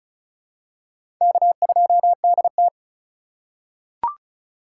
Morse code, heard as a keyed tone, about 35 wpm, 700 Hz. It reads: K2DT
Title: USA Call Signs - 4 Characters - 2x Character Spacing - 35wpm